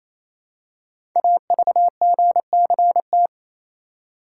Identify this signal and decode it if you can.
Morse code, heard as a keyed tone, about 28 wpm, 700 Hz. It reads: AVGCT
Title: Code Groups - 5 Letter:Numbers - 28wpm